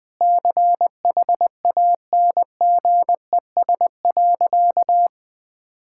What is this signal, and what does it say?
Morse code, heard as a keyed tone, about 20 wpm, 700 Hz. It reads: CHANGES.
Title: Sentences with 6-Word Phrases - 20wpm